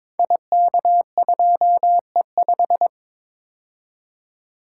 Morse code, heard as a keyed tone, about 22 wpm, 700 Hz. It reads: IK2E5
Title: Code Groups - 5 Letter:Numbers - 22wpm